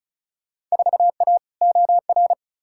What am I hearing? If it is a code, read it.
Morse code, heard as a keyed tone, about 36 wpm, 700 Hz. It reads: 4A OR
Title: Field Day Exchanges Encore - 35wpm